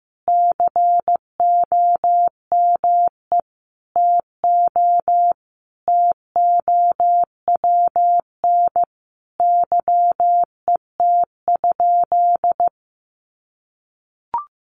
Morse code, heard as a keyed tone, about 15 wpm, 700 Hz. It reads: COME TO TOWN YET?
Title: Sentences from Top 1500 Words 15wpm